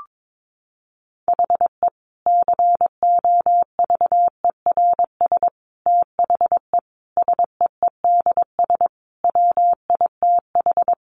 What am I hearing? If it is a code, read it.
Morse code, heard as a keyed tone, about 22 wpm, 700 Hz. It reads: HE COVERS THE SEEDS WITH
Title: Easy Phrases 22wpm